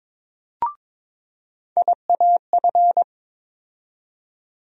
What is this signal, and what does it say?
Morse code, heard as a keyed tone, about 22 wpm, 700 Hz. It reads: IAF
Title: USA Call Sign Postfixes - 3 Letters 22wpm